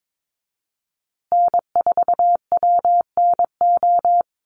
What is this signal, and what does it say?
Morse code, heard as a keyed tone, about 22 wpm, 700 Hz. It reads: N4WNO
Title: USA Call Signs Repeated 2x 22wpm